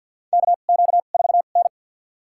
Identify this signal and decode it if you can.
Morse code, heard as a keyed tone, about 52 wpm, spaced as for 36 wpm, 700 Hz. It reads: KQ4N
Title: USA Call Signs - 4 Characters - 2x Character Spacing - 50wpm